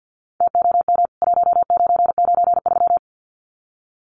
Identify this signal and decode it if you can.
Morse code, heard as a keyed tone, about 50 wpm, 700 Hz. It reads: TOM 1993
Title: CWT Exchange 50wpm